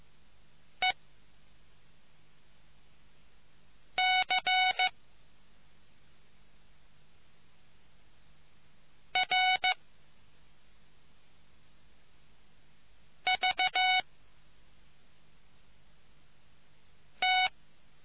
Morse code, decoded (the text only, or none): E C R V T